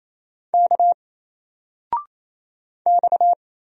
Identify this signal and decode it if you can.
Morse code, heard as a keyed tone, about 28 wpm, 700 Hz. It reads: K X
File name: Single-Letter-Number-28wpm